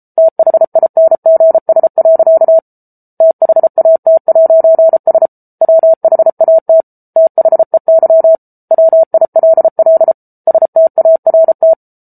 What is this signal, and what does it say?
Morse code, heard as a keyed tone, about 34 wpm, 650 Hz. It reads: THINGS. THAT'S WHAT THEY WILL START